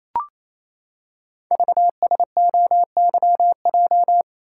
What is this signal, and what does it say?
Morse code, heard as a keyed tone, about 27 wpm, 700 Hz. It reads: VSOYJ
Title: Code Groups - 5 Letters - 28wpm